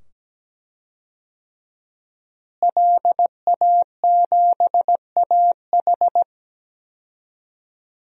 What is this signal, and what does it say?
Morse code, heard as a keyed tone, about 17 wpm, 700 Hz. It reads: LA7AH
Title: DXCC-100 Call Signs - 5 Characters - Repeated 2x - 17wpm